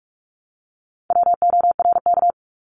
Morse code, heard as a keyed tone, about 46 wpm, 700 Hz. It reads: WORK
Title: Multi-letter Phonemes 45wpm